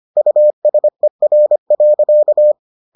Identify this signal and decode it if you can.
Morse code, heard as a keyed tone, about 25 wpm, 600 Hz. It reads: USER.